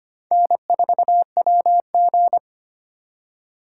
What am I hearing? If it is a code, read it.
Morse code, heard as a keyed tone, about 25 wpm, 700 Hz. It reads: N4WG
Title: USA Call Signs - 3x Character Spacing - 25wpm